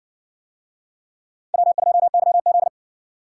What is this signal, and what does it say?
Morse code, heard as a keyed tone, about 58 wpm, 700 Hz. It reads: W2YZ